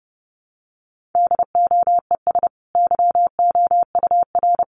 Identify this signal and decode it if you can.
Morse code, heard as a keyed tone, about 29 wpm, 700 Hz. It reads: DOES YOUR
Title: Sentences from Top 1400 Words 30wpm